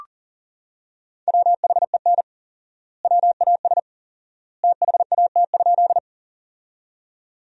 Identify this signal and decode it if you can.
Morse code, heard as a keyed tone, about 40 wpm, 700 Hz. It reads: WHEN WAS THAT?
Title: 4x Word-Spacing - Sentences from Top 100 Words - 40wpm